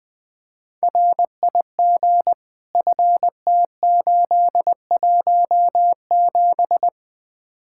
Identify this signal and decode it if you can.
Morse code, heard as a keyed tone, about 20 wpm, 700 Hz. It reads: RIG FT817
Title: Speed-Racing Common QSO Phrases - 20wpm